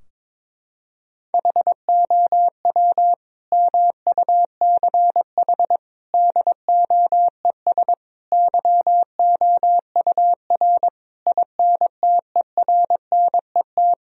Morse code, heard as a keyed tone, about 21 wpm, 700 Hz. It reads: HOW MUCH DOES YOUR INTERNET
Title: Sentences from Top 1000 Words - Review Part 4 - 22wpm